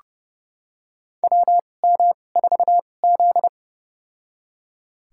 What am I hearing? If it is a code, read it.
Morse code, heard as a keyed tone, about 30 wpm, spaced as for 22 wpm, 700 Hz. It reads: WM4Z
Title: USA Call Signs - 4 Characters - 2x Character Spacing - 30wpm